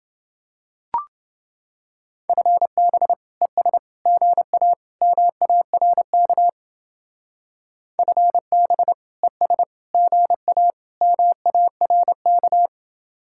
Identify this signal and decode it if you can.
Morse code, heard as a keyed tone, about 28 wpm, 700 Hz. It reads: FB ES GA MARK FB ES GA MARK